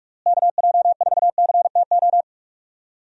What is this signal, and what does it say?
Morse code, heard as a keyed tone, about 46 wpm, 700 Hz. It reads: KJ4CTO
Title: USA Call Signs - 6 Characters - Repeated 2x - 45wpm